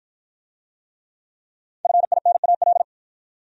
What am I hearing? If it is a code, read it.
Morse code, heard as a keyed tone, about 53 wpm, 700 Hz. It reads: FINAL